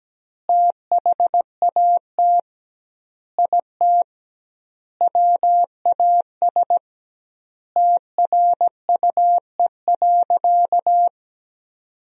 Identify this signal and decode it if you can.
Morse code, heard as a keyed tone, about 17 wpm, 700 Hz. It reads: THAT IT WAS TRUE.
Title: 2x Word-Spacing - Sentences from Top 400 Words - 17wpm